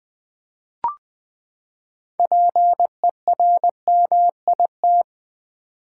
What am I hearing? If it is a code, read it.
Morse code, heard as a keyed tone, about 20 wpm, 700 Hz. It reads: PERMIT
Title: Six Letter Words - 20wpm